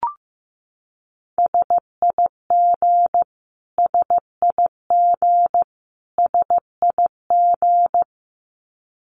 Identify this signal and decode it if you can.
Morse code, heard as a keyed tone, about 14 wpm, 700 Hz. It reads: SIG SIG SIG
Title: Abbreviations Repeated 3x - 15wpm